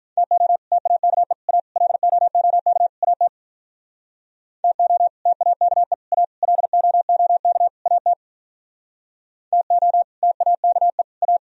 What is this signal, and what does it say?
Morse code, heard as a keyed tone, about 44 wpm, 700 Hz. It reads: TO TAKE A LOOK AT TO TAKE A LOOK AT TO TAKE A